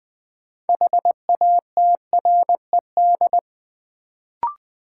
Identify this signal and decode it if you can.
Morse code, heard as a keyed tone, about 20 wpm, 700 Hz. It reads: HATRED
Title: Build A Word - 3 Letters At A Time - 6-Letter Words - 20wpm